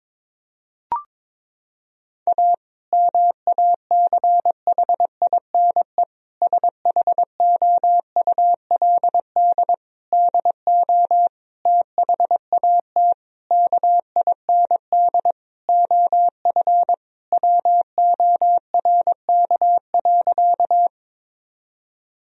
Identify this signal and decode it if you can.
Morse code, heard as a keyed tone, about 22 wpm, 700 Hz. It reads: A MACHINE SHOULD DO THAT KIND OF WORK.
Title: Sentences from Top 1000 Words - Review Part 2 - 22wpm